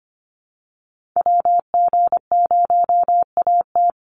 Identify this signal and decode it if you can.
Morse code, heard as a keyed tone, about 25 wpm, 700 Hz. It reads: WG0AT
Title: NA SOTA Activator Callsigns 25wpm